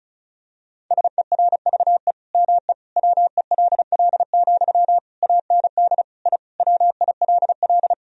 Morse code, heard as a keyed tone, about 34 wpm, 700 Hz. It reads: SERVE ME WELL, AND I WILL
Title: Sentences from Top 1000 Words - Review Part 2 - 35wpm